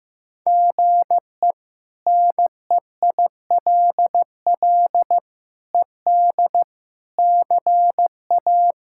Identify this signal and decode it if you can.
Morse code, heard as a keyed tone, about 15 wpm, 700 Hz. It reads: GE NEILL ED CA
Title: K1USN SST QSO Practice - 15wpm